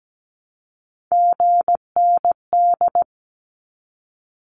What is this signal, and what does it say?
Morse code, heard as a keyed tone, about 17 wpm, 700 Hz. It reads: GND
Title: Abbreviations 17wpm